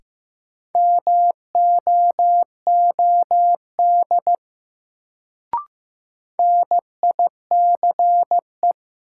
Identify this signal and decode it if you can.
Morse code, heard as a keyed tone, about 15 wpm, 700 Hz. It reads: MOOD NICE
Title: Sentences from Top 1100 Words - New Word Drill - 15wpm